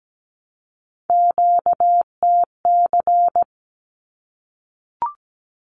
Morse code, heard as a keyed tone, about 17 wpm, 700 Hz. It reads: QTC
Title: Q-Codes 17wpm